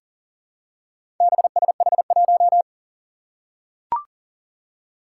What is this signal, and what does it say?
Morse code, heard as a keyed tone, about 40 wpm, 700 Hz. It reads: BSH1
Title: Four Challenging Letter-Number Combinations 40wpm